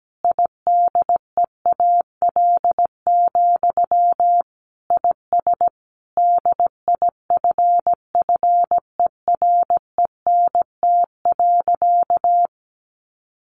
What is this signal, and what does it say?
Morse code, heard as a keyed tone, about 17 wpm, 700 Hz. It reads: IDEAL, IS DIFFERENT.
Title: Sentences from Top 1400 Words 17wpm